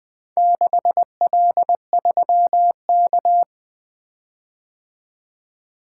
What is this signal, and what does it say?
Morse code, heard as a keyed tone, about 24 wpm, 700 Hz. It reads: 6L3K
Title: Speed-Racing - Four Letter-Number Combinations - 20wpm